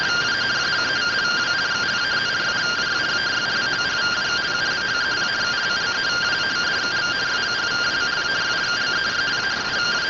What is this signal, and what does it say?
Two-tone teleprinter signal, about 170 Hz shift.
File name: findings / AXIAL fr1 3mm.gen-26 rtty rx